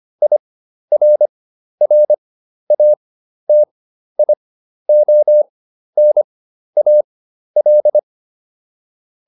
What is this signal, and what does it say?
Morse code, heard as a keyed tone, about 25 wpm, spaced as for 12 wpm, 600 Hz. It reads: IRRATIONAL